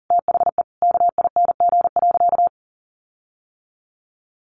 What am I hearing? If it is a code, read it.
Morse code, heard as a keyed tone, about 40 wpm, 700 Hz. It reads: THE KING.
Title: Sentences from Top 1400 Words 40wpm